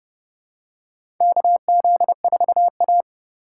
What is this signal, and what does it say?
Morse code, heard as a keyed tone, about 30 wpm, 700 Hz. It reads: KZ4A
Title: K1USN SST QSO Practice - 30wpm